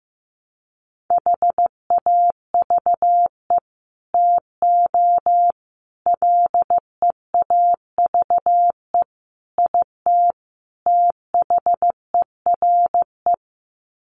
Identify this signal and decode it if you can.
Morse code, heard as a keyed tone, about 15 wpm, 700 Hz. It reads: HAVE TO LEAVE IT THERE
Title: Sets of 5 Words Encore2 15wpm